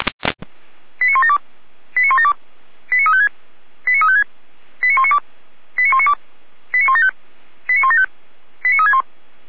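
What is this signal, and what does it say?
Five-tone selective call (ZVEI), heard as quick runs of five tones, tones of about 80 ms. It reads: 89172 89172 89356 89356 89192 89192 89175 89175 89361
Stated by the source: ZVEI five-tone call received on SDR.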